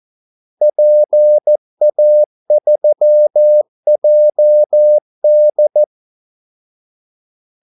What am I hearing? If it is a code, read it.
Morse code, heard as a keyed tone, about 14 wpm, 600 Hz. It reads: PA3JD
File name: Callsigns-14wpm